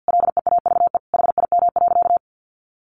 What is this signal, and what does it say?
Morse code, heard as a keyed tone, about 50 wpm, 700 Hz. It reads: LEAVE HIM.
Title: Sentences from Top 1500 Words 50wpm